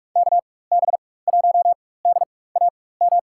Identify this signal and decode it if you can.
Morse code, heard as a keyed tone, about 47 wpm, spaced as for 21 wpm, 700 Hz. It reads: KB1DAM
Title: USA Call Signs - 6 Characters - 4x Character Spacing - 45wpm